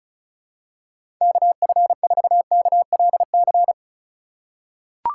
Morse code, heard as a keyed tone, about 36 wpm, 700 Hz. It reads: KF4KLC
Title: USA Call Signs - 6 Characters - 3x Character Spacing - 35wpm